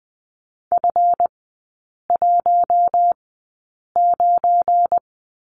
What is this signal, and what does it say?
Morse code, heard as a keyed tone, about 20 wpm, spaced as for 8 wpm, 700 Hz. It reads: F19
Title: 191 Lesson 31e - Letter v - Three Characters 20wpm